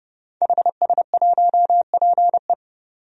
Morse code, heard as a keyed tone, about 30 wpm, 700 Hz. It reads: HS1PE
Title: DXCC-100 Call Signs - 5 Characters - 30wpm